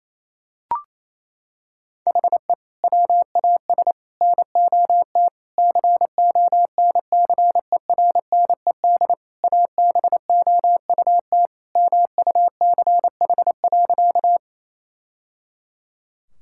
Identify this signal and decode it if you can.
Morse code, heard as a keyed tone, about 28 wpm, 700 Hz. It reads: HE WAS NOT CONCERNED ABOUT MUCH.
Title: Sentences from Top 1000 Words - Review Part 6 - 28wpm